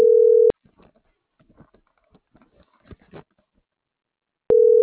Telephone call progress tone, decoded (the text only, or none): ringback tone